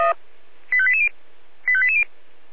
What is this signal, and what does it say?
Five-tone selective call (ZVEI), heard as quick runs of five tones, tones of about 70 ms. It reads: 85880 85880